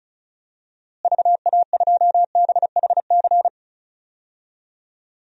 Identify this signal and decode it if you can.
Morse code, heard as a keyed tone, about 36 wpm, 700 Hz. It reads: VA2BHC
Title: DXCC-100 Call Signs - 6 Characters - Repeated 2x - 35wpm